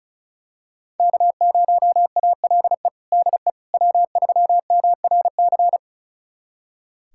Morse code, heard as a keyed tone, about 34 wpm, 700 Hz. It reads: K0ALE DE W3MRC
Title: USA Callsign DE Callsign - 5 Characters - 35wpm